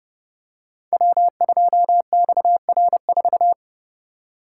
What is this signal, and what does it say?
Morse code, heard as a keyed tone, about 30 wpm, 700 Hz. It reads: W2XR4
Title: Code Groups - 5 Letter:Numbers - 30wpm